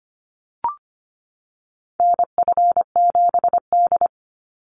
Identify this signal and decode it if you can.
Morse code, heard as a keyed tone, about 25 wpm, 700 Hz. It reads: NF7D
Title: Call Signs - 25wpm